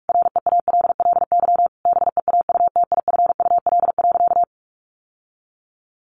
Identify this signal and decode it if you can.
Morse code, heard as a keyed tone, about 44 wpm, 700 Hz. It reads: REALLY BEAUTIFUL.
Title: Sentences from Top 1100 Words 45wpm